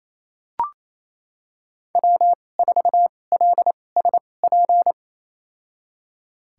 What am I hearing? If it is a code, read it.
Morse code, heard as a keyed tone, about 28 wpm, spaced as for 20 wpm, 700 Hz. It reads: W4LSP
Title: USA Call Signs - 5 Characters - 2x Character Spacing - 28wpm